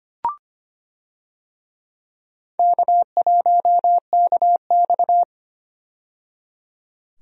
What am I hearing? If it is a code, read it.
Morse code, heard as a keyed tone, about 25 wpm, 700 Hz. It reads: K1KX
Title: USA Call Signs - 4 Characters - 25wpm